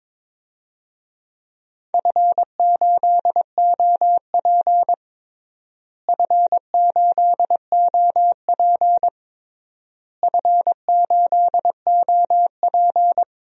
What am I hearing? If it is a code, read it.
Morse code, heard as a keyed tone, about 22 wpm, 700 Hz. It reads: F8OP F8OP F8OP
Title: DXCC-100 Call Signs - 4 Characters - Repeated 3x - 22wpm